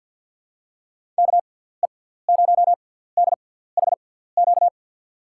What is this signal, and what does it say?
Morse code, heard as a keyed tone, about 50 wpm, spaced as for 17 wpm, 700 Hz. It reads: KE0DHQ
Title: USA Call Signs - 6 Characters - 6x Character Spacing - 50wpm